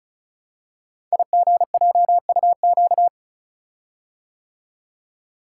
Morse code, heard as a keyed tone, about 36 wpm, 690 Hz. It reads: IGJUQ
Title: Code Groups - 5 Letters - 35wpm